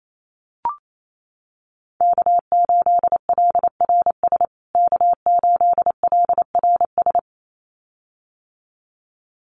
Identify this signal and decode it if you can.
Morse code, heard as a keyed tone, about 28 wpm, 700 Hz. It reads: K8LRS K8LRS
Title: USA Call Signs Repeated 2x 28wpm